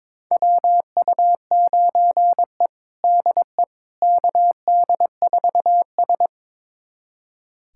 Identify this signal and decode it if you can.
Morse code, heard as a keyed tone, about 22 wpm, 700 Hz. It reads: WU9E DE KD4S
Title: USA Callsign DE Callsign - 4 Characters - 22wpm